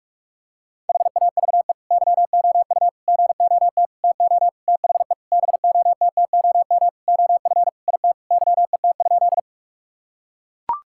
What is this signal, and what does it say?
Morse code, heard as a keyed tone, about 42 wpm, 700 Hz. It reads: HAVE YOU GOT TO THE BOTTOM OF IT YET?
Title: Sentences from Top 1000 Words - Review Part 6 - 45wpm